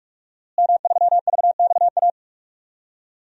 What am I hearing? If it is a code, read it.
Morse code, heard as a keyed tone, about 46 wpm, 700 Hz. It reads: M3VXA
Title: DXCC-100 Call Signs - 5 Characters - Repeated 2x - 45wpm